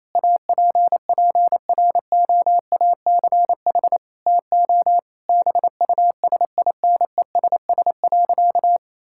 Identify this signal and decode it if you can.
Morse code, heard as a keyed tone, about 28 wpm, 700 Hz. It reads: APPROACH TO BUSINESS.
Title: Sentences from Top 1000 Words - Review Part 6 - 28wpm